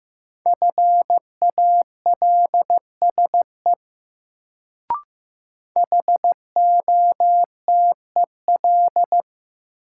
Morse code, heard as a keyed tone, about 15 wpm, 700 Hz. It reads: FALSE HOTEL
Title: Sentences from Top 1600 Words - New Word Drill - 15wpm